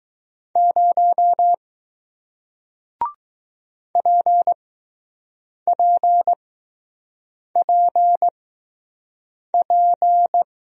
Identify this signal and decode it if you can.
Morse code, heard as a keyed tone, about 21 wpm, spaced as for 6 wpm, 700 Hz. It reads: 0 PPPP